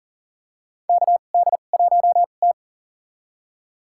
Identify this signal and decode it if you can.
Morse code, heard as a keyed tone, about 37 wpm, spaced as for 28 wpm, 700 Hz. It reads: KD1T
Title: USA Call Signs - 4 Characters - 2x Character Spacing - 40wpm